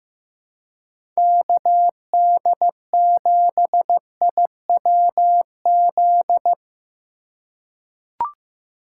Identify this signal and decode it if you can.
Morse code, heard as a keyed tone, about 15 wpm, 700 Hz. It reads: KD7IWZ
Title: USA Call Signs - 6 Characters - 2x Character Spacing - 15wpm